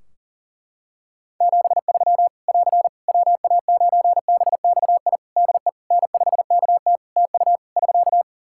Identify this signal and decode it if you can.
Morse code, heard as a keyed tone, about 37 wpm, 700 Hz. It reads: 73 <AR> WA9BXI DE N5KT TU <SK>
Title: Practice QSO - Exchange 4 - Ending - 40wpm